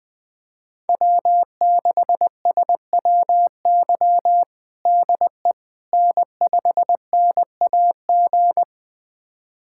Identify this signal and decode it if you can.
Morse code, heard as a keyed tone, about 21 wpm, 700 Hz. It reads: W6SWY DE N5NAG
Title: USA Callsign DE Callsign 20wpm